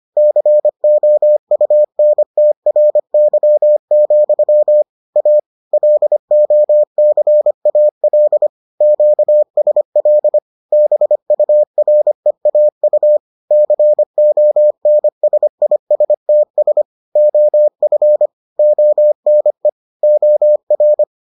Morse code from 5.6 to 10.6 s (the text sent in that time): LOCAL QSL